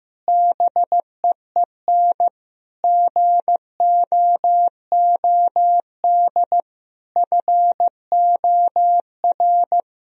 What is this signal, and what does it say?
Morse code, heard as a keyed tone, about 15 wpm, 700 Hz. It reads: BEEN GOOD FOR
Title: Sentences from Top 1000 Words - Review Part 4 - 15wpm